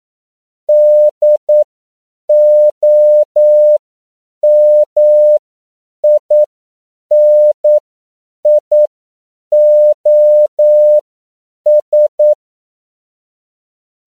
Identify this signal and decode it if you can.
Morse code, heard as a keyed tone, about 9 wpm, spaced as for 7 wpm, 600 Hz. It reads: DOMINIOS